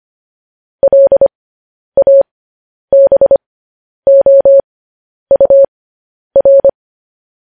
Morse code, heard as a keyed tone, about 25 wpm, spaced as for 10 wpm, 550 Hz. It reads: LABOUR
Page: news headlines in morse code at 10 wpm